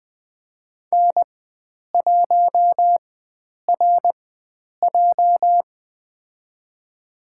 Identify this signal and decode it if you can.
Morse code, heard as a keyed tone, about 20 wpm, spaced as for 9 wpm, 700 Hz. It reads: N1RJ
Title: USA Call Signs - 4 Characters - 4x Character Spacing - 20wpm